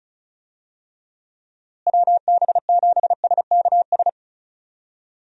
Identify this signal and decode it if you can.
Morse code, heard as a keyed tone, about 36 wpm, 700 Hz. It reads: WB7SKS
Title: USA Call Signs - 6 Characters - Repeated 3x - 35wpm